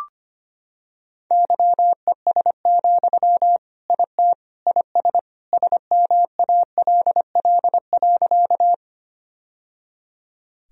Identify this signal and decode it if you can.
Morse code, heard as a keyed tone, about 25 wpm, 700 Hz. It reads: YES, IT IS SMALL.
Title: Sentences from Top 1000 Words - Review Part 1 - 25wpm